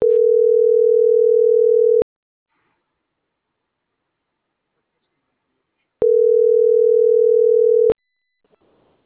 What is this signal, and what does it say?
Ringback tone (North America), 440 + 480 Hz.
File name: CENTRAL MAERSK / out jesus.madrigal